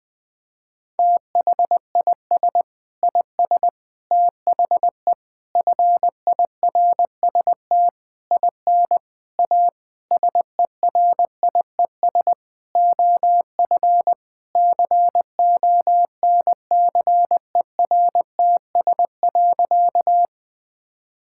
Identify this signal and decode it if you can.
Morse code, heard as a keyed tone, about 20 wpm, 700 Hz. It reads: THIS IS THE FIRST IN A SERIES OF CONCERTS.
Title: Sentences with 8-Word Phrases - 20wpm